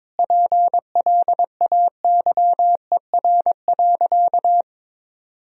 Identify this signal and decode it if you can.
Morse code, heard as a keyed tone, about 22 wpm, 700 Hz. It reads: PLAYER.
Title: Sentences from Top 1000 Words - Review Part 4 - 22wpm